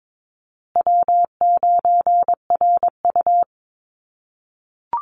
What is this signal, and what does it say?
Morse code, heard as a keyed tone, about 22 wpm, 700 Hz. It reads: W9RU
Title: USA Call Signs Repeated 2x 22wpm